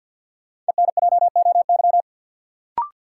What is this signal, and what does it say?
Morse code, heard as a keyed tone, about 50 wpm, 700 Hz. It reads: ENJOY
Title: Sentences from Top 1000 Words - Review Part 4 - New Word Drill - 50wpm